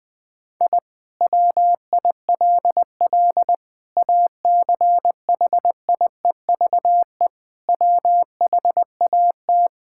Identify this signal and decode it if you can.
Morse code, heard as a keyed tone, about 20 wpm, 700 Hz. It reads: I WILL ACHIEVE WHAT
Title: Sentences from Top 1000 Words - Review Part 6 - 20wpm